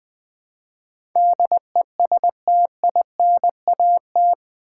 Morse code, heard as a keyed tone, about 20 wpm, 700 Hz. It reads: DESTINAT
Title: Build A Word - 4 Letters At A Time - 11-Letter Words - 20wpm